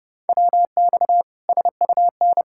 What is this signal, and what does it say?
Morse code, heard as a keyed tone, about 30 wpm, 700 Hz. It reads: WX SUN
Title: Practice QSO - Exchange 2 - Info - 30wpm